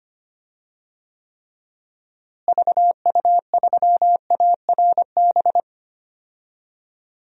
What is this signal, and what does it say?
Morse code, heard as a keyed tone, about 25 wpm, 700 Hz. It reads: VU3ARB
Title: DXCC-100 Call Signs - 6 Characters - Repeated 2x - 25wpm